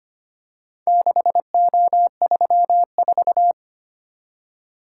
Morse code, heard as a keyed tone, about 25 wpm, 700 Hz. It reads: 6O34
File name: practice / 25wpm / Four Letter-Number Combinations - 25wpm